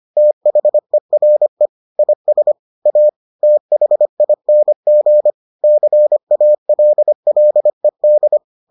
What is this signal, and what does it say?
Morse code, heard as a keyed tone, about 25 wpm, 600 Hz. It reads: THERE IS A THING CALLED